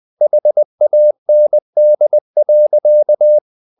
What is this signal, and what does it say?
Morse code, heard as a keyed tone, about 20 wpm, 600 Hz. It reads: HAND.
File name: the old man and the sea / the old man and the sea part 1 20wpm